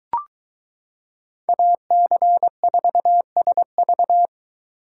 Morse code, heard as a keyed tone, about 22 wpm, 700 Hz. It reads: AC4SV